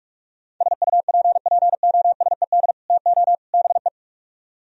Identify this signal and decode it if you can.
Morse code, heard as a keyed tone, about 49 wpm, 700 Hz. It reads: SUPPOSED TO BE